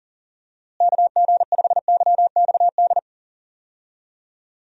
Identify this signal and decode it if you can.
Morse code, heard as a keyed tone, about 37 wpm, 700 Hz. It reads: KG5YXD
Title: USA Call Signs - 6 Characters - Repeated 3x - 40wpm